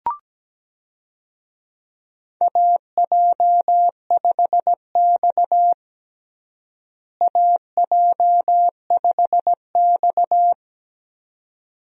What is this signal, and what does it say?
Morse code, heard as a keyed tone, about 17 wpm, 700 Hz. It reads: AJ5X AJ5X
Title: USA Call Signs - 4 Characters - Repeated 2x - 17wpm